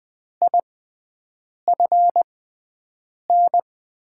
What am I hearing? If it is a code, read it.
Morse code, heard as a keyed tone, about 20 wpm, spaced as for 7 wpm, 700 Hz. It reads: IFN
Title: USA Call Signs - 6x Character Spacing - 20wpm